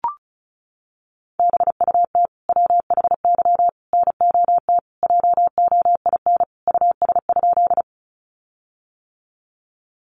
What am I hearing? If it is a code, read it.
Morse code, heard as a keyed tone, about 34 wpm, 700 Hz. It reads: BUT WHY NOT JOIN US?
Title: Sentences from Top 1400 Words 35wpm